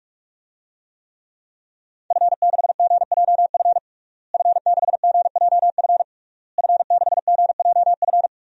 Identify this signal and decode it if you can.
Morse code, heard as a keyed tone, about 42 wpm, 700 Hz. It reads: F6GJF F6GJF F6GJF
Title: DXCC-100 Call Signs - 5 Characters - Repeated 3x - 45wpm